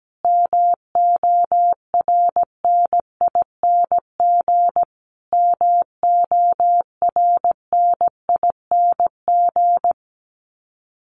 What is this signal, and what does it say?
Morse code, heard as a keyed tone, about 17 wpm, 700 Hz. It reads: MORNING MORNING
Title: Top 500 Words Repeated 17wpm